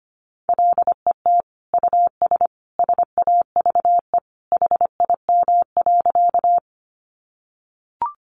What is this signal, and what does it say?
Morse code, heard as a keyed tone, about 25 wpm, 700 Hz. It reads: LET US SAVE HIM.
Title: Sentences from Top 1100 Words 25wpm